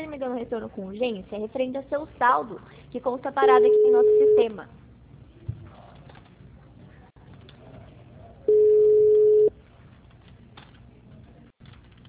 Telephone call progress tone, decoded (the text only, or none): ringback tone